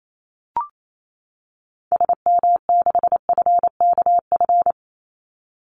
Morse code, heard as a keyed tone, about 28 wpm, 700 Hz. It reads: SM6FKF
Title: Call Signs - 28wpm